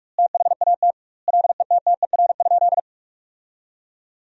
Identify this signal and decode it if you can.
Morse code, heard as a keyed tone, about 46 wpm, 700 Hz. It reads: THAT LETTER?
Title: 2x Word-Spacing - Sentences from Top 300 Words - 45wpm